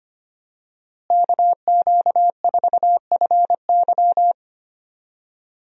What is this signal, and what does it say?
Morse code, heard as a keyed tone, about 25 wpm, 700 Hz. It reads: KQ4FY
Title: USA Call Signs - 5 Characters - Repeated 2x - 25wpm